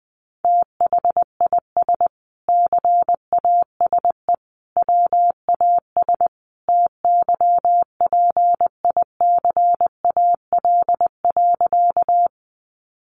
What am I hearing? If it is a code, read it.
Morse code, heard as a keyed tone, about 20 wpm, 700 Hz. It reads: THIS CASE WAS TYPICAL.
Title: Sentences from Top 1100 Words 20wpm